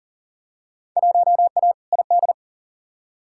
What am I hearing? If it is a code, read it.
Morse code, heard as a keyed tone, about 39 wpm, 700 Hz. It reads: 1A ID
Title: Field Day Exchanges - 40wpm